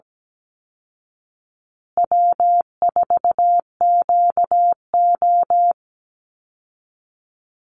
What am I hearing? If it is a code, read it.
Morse code, heard as a keyed tone, about 17 wpm, 700 Hz. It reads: W4QO
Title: Easy Call Signs Repeated 3x - 17wpm